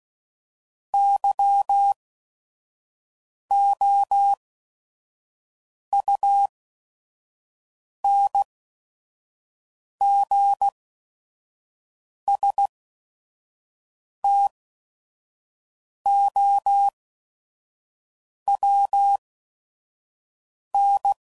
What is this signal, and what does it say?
Morse code, heard as a keyed tone, about 16 wpm, spaced as for 5 wpm, 800 Hz. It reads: YOUNGSTOWN
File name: CW Code Groups @5WPM / fromSM's / Cities